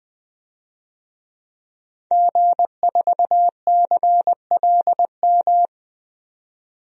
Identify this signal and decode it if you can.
Morse code, heard as a keyed tone, about 20 wpm, 700 Hz. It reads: G4CLM
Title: DXCC-100 Call Signs - 5 Characters - 20wpm